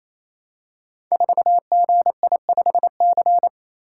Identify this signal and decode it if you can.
Morse code, heard as a keyed tone, about 27 wpm, 700 Hz. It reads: 4GI5C
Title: Code Groups - 5 Letter:Numbers - 28wpm